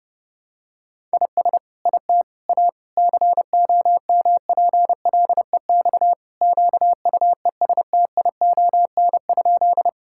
Morse code, heard as a keyed tone, about 30 wpm, 700 Hz. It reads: IS IT A COMPLEX QUESTION?
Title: Sentences from Top 1000 Words - Review Part 6 - 30wpm